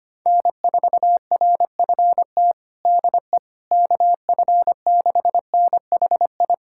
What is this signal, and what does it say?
Morse code, heard as a keyed tone, about 25 wpm, 700 Hz. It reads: N4RFT DE KF6NHI
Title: USA Callsign DE Callsign 25wpm